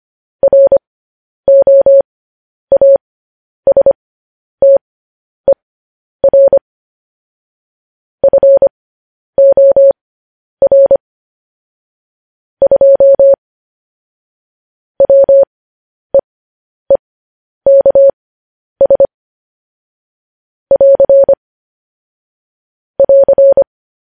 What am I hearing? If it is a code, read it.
Morse code, heard as a keyed tone, about 25 wpm, spaced as for 10 wpm, 550 Hz. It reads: ROASTER FOR 2 WEEKS <AR> <AR>
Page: news-headlines-in-morse-code-at-10-wpm